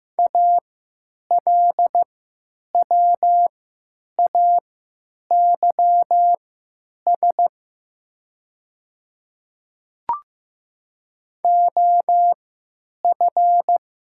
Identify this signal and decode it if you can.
Morse code, heard as a keyed tone, about 15 wpm, spaced as for 9 wpm, 700 Hz. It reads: ALWAYS OF
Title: 3x Character Spacing - Top 300 Words - 15wpm